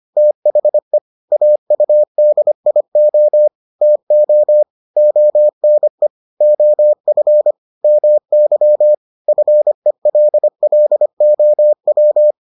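Morse code, heard as a keyed tone, about 25 wpm, 600 Hz. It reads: THE AUDIO TO ONE OF MY FELLOW
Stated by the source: Morse edition of the Foundations of Amateur Radio episodes.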